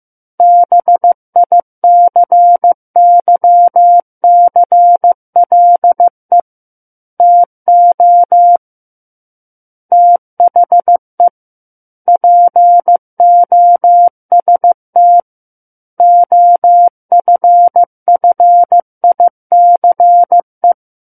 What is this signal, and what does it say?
Morse code, heard as a keyed tone, about 15 wpm, 700 Hz. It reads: BICYCLE TO THE POST OFFICE
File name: Book One / 06-Chapter 6 - 15 WPM